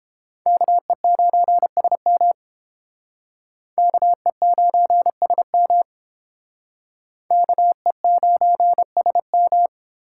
Morse code, heard as a keyed tone, about 30 wpm, 700 Hz. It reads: KE9SM KE9SM KE9SM